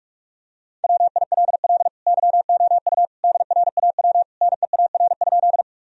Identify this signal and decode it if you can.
Morse code, heard as a keyed tone, about 41 wpm, 700 Hz. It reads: WILL YOU DRAW NEAR?
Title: Sentences from Top 1000 Words - Review Part 1 - 45wpm